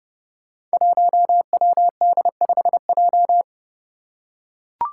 Morse code, heard as a keyed tone, about 30 wpm, 700 Hz. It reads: 1WD5J
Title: Five Challenging Letter-Number Combinations 30wpm